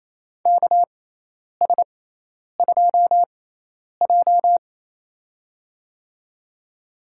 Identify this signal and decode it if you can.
Morse code, heard as a keyed tone, about 28 wpm, spaced as for 10 wpm, 700 Hz. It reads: KS2J
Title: USA Call Signs - 4 Characters - 6x Character Spacing - 28wpm